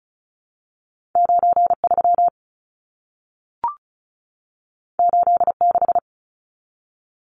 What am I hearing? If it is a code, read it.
Morse code, heard as a keyed tone, about 34 wpm, 700 Hz. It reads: 93 86
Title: Numbers - 2-Digit 35wpm